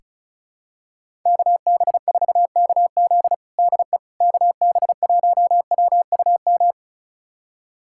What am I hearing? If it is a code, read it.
Morse code, heard as a keyed tone, about 36 wpm, 700 Hz. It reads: KB4KZ DE KB1WUM
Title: USA Callsign DE Callsign 35wpm